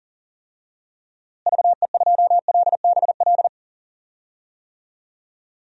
Morse code, heard as a keyed tone, about 40 wpm, 700 Hz. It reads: VE2LBL
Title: DXCC-100 Call Signs - 6 Characters - Repeated 3x - 40wpm